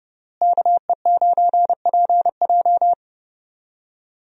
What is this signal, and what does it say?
Morse code, heard as a keyed tone, about 30 wpm, 700 Hz. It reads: KE9PJ
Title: USA Call Signs - 5 Characters - Repeated 2x - 30wpm